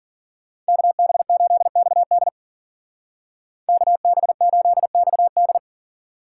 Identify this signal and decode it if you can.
Morse code, heard as a keyed tone, about 45 wpm, 700 Hz. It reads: KB8XD KB8XD
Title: Speed Racing - USA Call Signs 40wpm